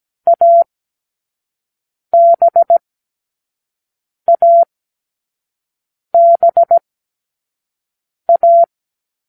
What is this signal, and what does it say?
Morse code, heard as a keyed tone, about 17 wpm, spaced as for 5 wpm, 680 Hz. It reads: ABABA